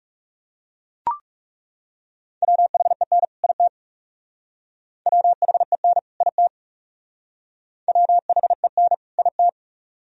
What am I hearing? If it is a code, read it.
Morse code, heard as a keyed tone, about 36 wpm, 700 Hz. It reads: WHEN IT WHEN IT WHEN IT
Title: Speed-Racing - Sets of 2 Words 30wpm